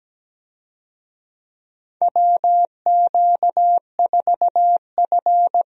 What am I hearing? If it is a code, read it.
Morse code, heard as a keyed tone, about 17 wpm, 700 Hz. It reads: WQ4F
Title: USA Call Signs - 4 Characters - Repeated 2x - 17wpm